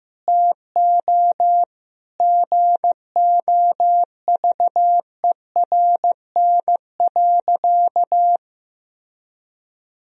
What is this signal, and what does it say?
Morse code, heard as a keyed tone, about 15 wpm, 700 Hz. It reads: TO GOVERN.
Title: Sentences from Top 1000 Words - Review Part 2 - 15wpm